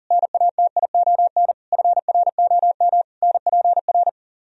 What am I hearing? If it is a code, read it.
Morse code, heard as a keyed tone, about 50 wpm, 700 Hz. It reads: NATION FROM NPR
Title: Speed-Racing - Sets of 5 Words 40wpm